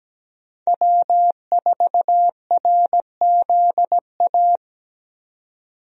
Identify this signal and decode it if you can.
Morse code, heard as a keyed tone, about 17 wpm, 700 Hz. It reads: W4RZA
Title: USA Call Signs - 5 Characters - 17wpm